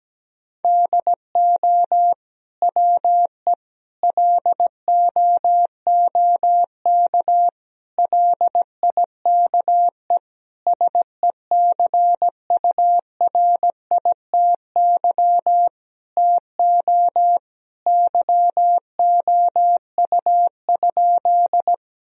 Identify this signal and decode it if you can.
Morse code, heard as a keyed tone, about 17 wpm, 700 Hz. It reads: DO WE LOOK LIKE SECURITY TO YOU?
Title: Sentences from Top 1000 Words - Review Part 4 - 17wpm